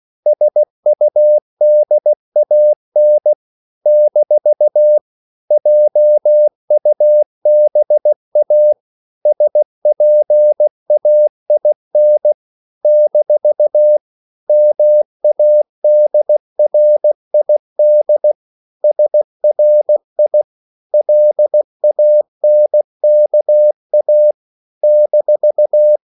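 Morse code, heard as a keyed tone, about 16 wpm, 600 Hz. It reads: SUDAN - JUBA SPAIN - MADRID SRI LANKA -
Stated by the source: CW practice of national capitals at 16 words per minute, from ON6ZQ.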